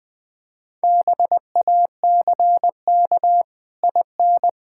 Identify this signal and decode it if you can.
Morse code, heard as a keyed tone, about 21 wpm, 700 Hz. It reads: BACK IN
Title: Build A Phrase - 3-Word Phrases - 20wpm